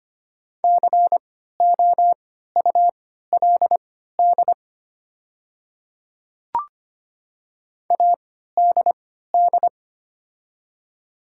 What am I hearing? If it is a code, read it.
Morse code, heard as a keyed tone, about 25 wpm, spaced as for 14 wpm, 700 Hz. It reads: COULD ADD
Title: 3x Character Spacing - Top 300 Words - 25wpm